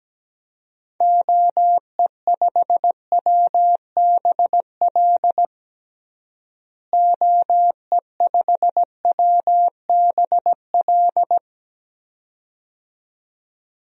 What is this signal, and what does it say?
Morse code, heard as a keyed tone, about 17 wpm, 700 Hz. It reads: OE5WBL OE5WBL
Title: DXCC-100 Call Signs - 6 Characters - Repeated 2x - 17wpm